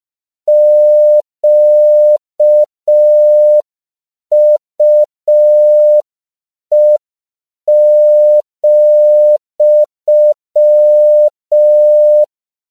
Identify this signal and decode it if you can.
Morse code, heard as a keyed tone, about 5 wpm, 600 Hz. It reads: QUE,
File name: CW / 5PPM